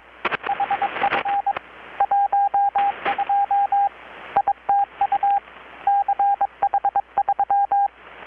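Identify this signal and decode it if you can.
Morse code, heard as a keyed tone, about 23 wpm, 800 Hz, with radio noise. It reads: HF 12 ITU CH3
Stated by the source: Morse from KPH's Night of Nights XX maritime radiotelegraphy event, received in Chiapas, Mexico about 3,800 km away on a Kenwood R-600 with a grounded 10 m long-wire antenna.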